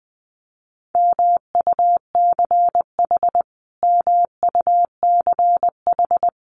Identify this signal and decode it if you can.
Morse code, heard as a keyed tone, about 20 wpm, 700 Hz. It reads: MUCH MUCH
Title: Top 500 Words Repeated 20wpm